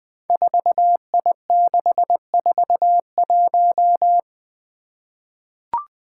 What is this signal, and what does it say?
Morse code, heard as a keyed tone, about 20 wpm, 700 Hz. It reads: 4I641
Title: Five Challenging Letter-Number Combinations 20wpm